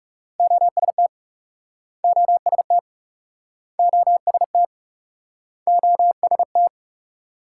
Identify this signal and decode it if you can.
Morse code, heard as a keyed tone, about 39 wpm, 700 Hz. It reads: OST OST OST OST